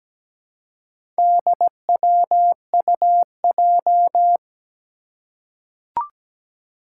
Morse code, heard as a keyed tone, about 17 wpm, 700 Hz. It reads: DWUJ
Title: Four Challenging Letter-Number Combinations 17wpm